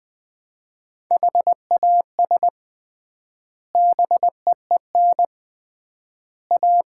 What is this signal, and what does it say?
Morse code, heard as a keyed tone, about 20 wpm, 700 Hz. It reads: HAS BEEN A
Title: 3x Word-Spacing - Sets of 3 Words - 20wpm